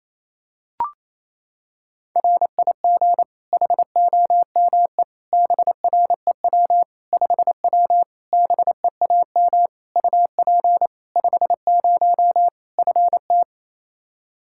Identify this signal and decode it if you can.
Morse code, heard as a keyed tone, about 28 wpm, 700 Hz. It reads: RIG HOME BREW 5W BEAM UP 50 FT
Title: Practice QSO - Exchange 2 - Info - 28wpm